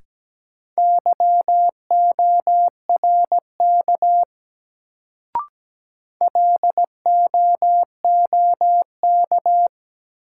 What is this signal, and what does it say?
Morse code, heard as a keyed tone, about 17 wpm, 700 Hz. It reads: YORK LOOK
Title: Sets of 2 Words - New Word Drill - 17wpm